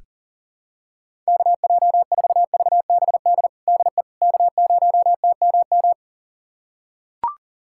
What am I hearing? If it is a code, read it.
Morse code, heard as a keyed tone, about 37 wpm, 700 Hz. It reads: KJ4VBD DE K0TMM
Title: USA Callsign DE Callsign 40wpm